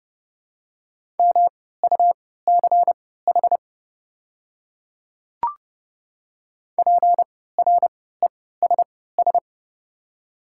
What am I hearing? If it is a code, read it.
Morse code, heard as a keyed tone, about 32 wpm, spaced as for 17 wpm, 700 Hz. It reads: MUCH PRESS
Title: 3x Character Spacing - Top 300 Words - 30wpm